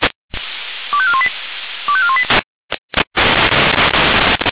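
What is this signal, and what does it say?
Five-tone selective call (ZVEI), heard as quick runs of five tones, tones of about 70 ms. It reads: 25418 25418
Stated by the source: Received on an SDR.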